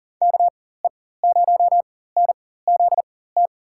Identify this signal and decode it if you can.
Morse code, heard as a keyed tone, about 40 wpm, spaced as for 19 wpm, 700 Hz. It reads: KE0NZT
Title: USA Call Signs - 6 Characters - 4x Character Spacing - 40wpm